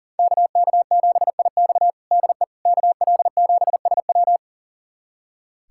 Morse code, heard as a keyed tone, about 40 wpm, 700 Hz. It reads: KK7IX DE KL7SW
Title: USA Callsign DE Callsign - 5 Characters - 40wpm